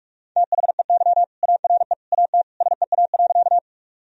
Morse code, heard as a keyed tone, about 44 wpm, 700 Hz. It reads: THEY ARE AT SEA.
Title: Sentences from Top 1000 Words - Review Part 1 - 45wpm